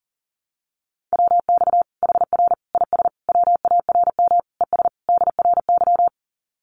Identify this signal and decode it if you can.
Morse code, heard as a keyed tone, about 40 wpm, 700 Hz. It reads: WX HR IS WARM ES DRY
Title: Common QSO Phrases 40wpm